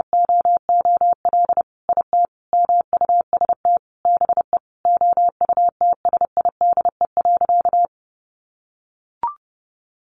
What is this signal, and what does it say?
Morse code, heard as a keyed tone, about 30 wpm, 700 Hz. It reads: OOL IT MUST BE OUTSIDE.
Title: Sentences from Top 1200 Words 30wpm